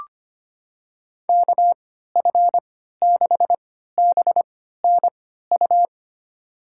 Morse code, heard as a keyed tone, about 25 wpm, spaced as for 14 wpm, 700 Hz. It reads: KF6BNU
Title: USA Call Signs - 6 Characters - 3x Character Spacing - 25wpm